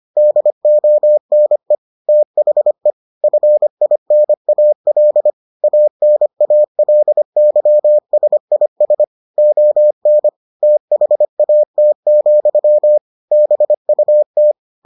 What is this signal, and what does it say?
Morse code, heard as a keyed tone, about 25 wpm, 600 Hz. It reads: DONE THE FINAL ANALYSIS ON THAT, BUT